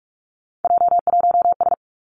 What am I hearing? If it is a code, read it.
Morse code, heard as a keyed tone, about 46 wpm, 700 Hz. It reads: J1S
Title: Three Challenging Letter-Number Combinations 45wpm